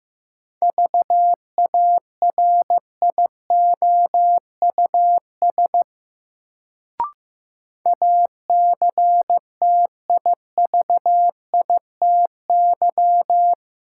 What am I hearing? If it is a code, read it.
Morse code, heard as a keyed tone, about 15 wpm, 700 Hz. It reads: VARIOUS ACTIVITY
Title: Sentences from Top 1000 Words - Review Part 4 - New Word Drill - 15wpm